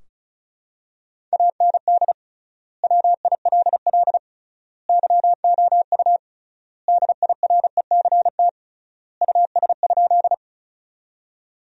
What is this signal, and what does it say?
Morse code, heard as a keyed tone, about 34 wpm, 700 Hz. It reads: AND WILL YOU DIRECT US?
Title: 3x Word-Spacing - Sentences from Top 400 Words - 35wpm